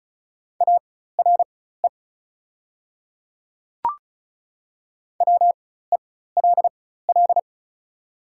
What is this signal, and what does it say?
Morse code, heard as a keyed tone, about 36 wpm, spaced as for 16 wpm, 700 Hz. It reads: ARE WELL
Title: 4x Character Spacing - Top 200 Words - 35wpm